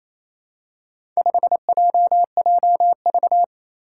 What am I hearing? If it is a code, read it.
Morse code, heard as a keyed tone, about 27 wpm, 700 Hz. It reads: 5JJV
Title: Four Challenging Letter-Number Combinations 28wpm